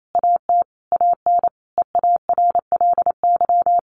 Morse code, heard as a keyed tone, about 28 wpm, 700 Hz. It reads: AT AN EARLY
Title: Sentences from Top 1200 Words 28wpm